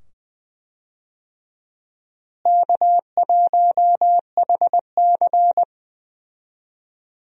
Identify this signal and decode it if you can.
Morse code, heard as a keyed tone, about 20 wpm, 700 Hz. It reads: K1HC
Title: USA Call Signs - 4 Characters - Repeated 2x - 20wpm